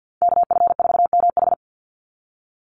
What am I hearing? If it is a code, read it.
Morse code, heard as a keyed tone, about 50 wpm, 700 Hz. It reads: KF4MH
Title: NA SOTA Chaser Callsigns 50wpm